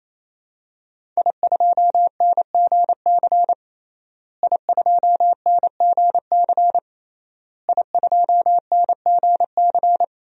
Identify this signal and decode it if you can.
Morse code, heard as a keyed tone, about 28 wpm, 700 Hz. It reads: I2NGC I2NGC I2NGC
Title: DXCC-100 Call Signs - 5 Characters - Repeated 3x - 28wpm